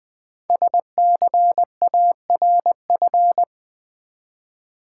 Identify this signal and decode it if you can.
Morse code, heard as a keyed tone, about 20 wpm, 700 Hz. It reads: SCARF
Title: Easy Words with Hints - 20wpm